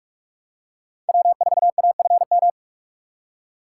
Morse code, heard as a keyed tone, about 46 wpm, 700 Hz. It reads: W4AFM
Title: USA Call Signs - 3x Character Spacing - 45wpm